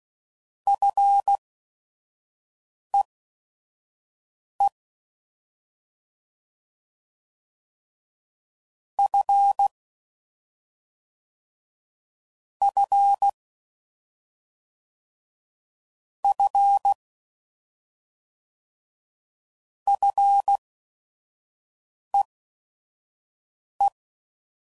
Morse code, heard as a keyed tone, about 16 wpm, spaced as for 5 wpm, 800 Hz. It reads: FEE F F F FEE
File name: Prefixes / FLs & THs / Single Fs Ls